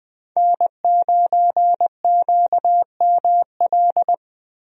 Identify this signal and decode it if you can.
Morse code, heard as a keyed tone, about 21 wpm, 700 Hz. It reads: N9QML
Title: USA Call Signs - 5 Characters - Repeated 2x - 20wpm